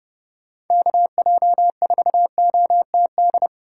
Morse code, heard as a keyed tone, about 28 wpm, 700 Hz. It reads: KJ4OTD